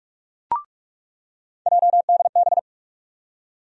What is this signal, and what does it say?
Morse code, heard as a keyed tone, about 42 wpm, 700 Hz. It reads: JDB
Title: USA Call Sign Postfixes - 3 Letters 45wpm